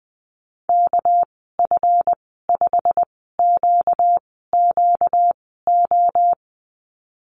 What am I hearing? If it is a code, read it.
Morse code, heard as a keyed tone, about 20 wpm, spaced as for 15 wpm, 700 Hz. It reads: KF5QQO
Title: 238 Lesson 38g - Letter q - Call Signs 20wpm